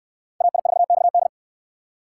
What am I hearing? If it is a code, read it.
Morse code, heard as a keyed tone, about 68 wpm, 700 Hz. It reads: AI4XD